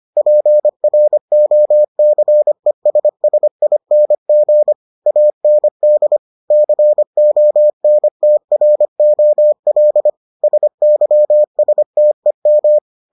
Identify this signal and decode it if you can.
Morse code, heard as a keyed tone, about 25 wpm, 600 Hz. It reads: PROCESSING AND CONTROL SYSTEM